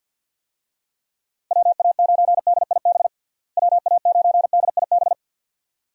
Morse code, heard as a keyed tone, about 46 wpm, 700 Hz. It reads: WA9DIB WA9DIB
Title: USA Call Signs - 6 Characters - Repeated 2x - 50wpm